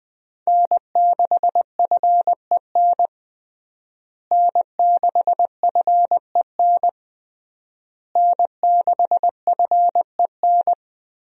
Morse code, heard as a keyed tone, about 20 wpm, 700 Hz. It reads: N6FEN N6FEN N6FEN
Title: USA Call Signs - 5 Characters - Repeated 3x - 20wpm